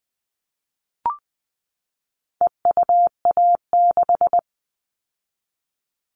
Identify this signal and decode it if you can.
Morse code, heard as a keyed tone, about 20 wpm, 700 Hz. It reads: EUA6
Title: All Cut Numbers - 4-Digit 20wpm